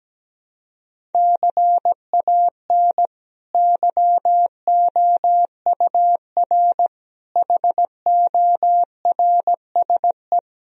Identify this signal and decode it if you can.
Morse code, heard as a keyed tone, about 17 wpm, 700 Hz. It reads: CAN YOUR HORSE
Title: Sentences from Top 1000 Words - Review Part 1 - 17wpm